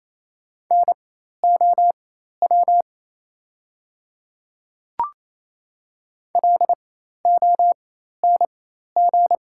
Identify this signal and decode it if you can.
Morse code, heard as a keyed tone, about 27 wpm, spaced as for 13 wpm, 690 Hz. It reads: NOW LONG
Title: 4x Character Spacing - Top 100 Words - 28wpm